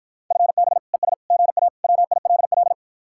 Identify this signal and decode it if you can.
Morse code, heard as a keyed tone, about 54 wpm, 700 Hz. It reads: FB ES GA WILL